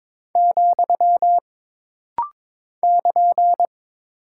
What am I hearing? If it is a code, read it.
Morse code, heard as a keyed tone, about 22 wpm, 700 Hz. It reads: , <KN>
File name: practice / 22wpm / Single Punctuation-Prosign - Rapid-Fire - 22wpm